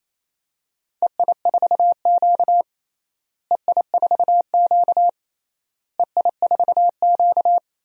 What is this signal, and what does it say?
Morse code, heard as a keyed tone, about 27 wpm, 700 Hz. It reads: EI4Q EI4Q EI4Q
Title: DXCC-100 Call Signs - 4 Characters - Repeated 3x - 28wpm